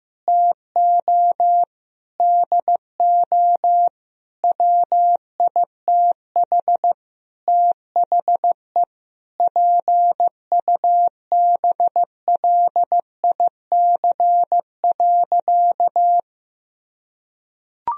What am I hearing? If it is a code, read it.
Morse code, heard as a keyed tone, about 16 wpm, 700 Hz. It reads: TO DO WITH THE PUBLIC.
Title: Sentences from Top 1000 Words - Review Part 3 - 15wpm